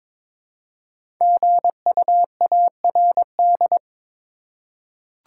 Morse code, heard as a keyed tone, about 22 wpm, 700 Hz. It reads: GUARD
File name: 5 Letter Words - First 3 Letters Given - 22wpm